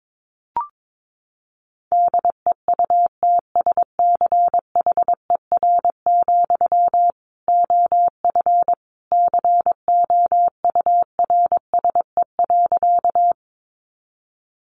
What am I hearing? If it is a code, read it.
Morse code, heard as a keyed tone, about 22 wpm, 700 Hz. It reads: DEUTSCHER, OF COURSE.
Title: A Sound Of Thunder 22wpm